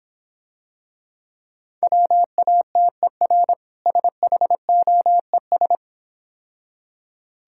Morse code, heard as a keyed tone, about 26 wpm, 700 Hz. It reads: WATER SHOES